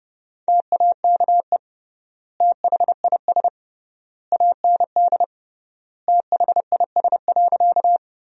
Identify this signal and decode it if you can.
Morse code, heard as a keyed tone, about 30 wpm, 700 Hz. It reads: TAKE THIS AND THIS.
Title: 3x Word-Spacing - Sentences from Top 200 Words - 30wpm